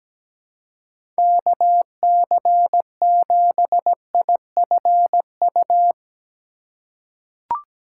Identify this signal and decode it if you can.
Morse code, heard as a keyed tone, about 17 wpm, 700 Hz. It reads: KC7IFU
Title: USA Call Signs - 6 Characters - 2x Character Spacing - 17wpm